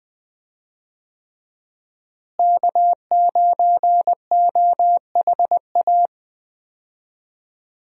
Morse code, heard as a keyed tone, about 20 wpm, 700 Hz. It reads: K9OHA
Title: USA Call Signs - 5 Characters - Repeated 3x - 20wpm